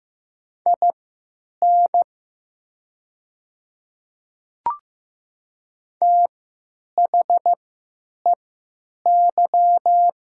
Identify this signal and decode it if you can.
Morse code, heard as a keyed tone, about 15 wpm, spaced as for 9 wpm, 700 Hz. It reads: IN THEY
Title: 3x Character Spacing - Top 100 Words - 15wpm